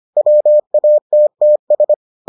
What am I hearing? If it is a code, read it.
Morse code, heard as a keyed tone, about 25 wpm, 600 Hz. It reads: WATTS